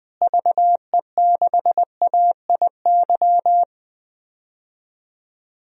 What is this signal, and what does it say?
Morse code, heard as a keyed tone, about 20 wpm, 700 Hz. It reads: VE6AIY
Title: DXCC-100 Call Signs - 6 Characters - Repeated 3x - 20wpm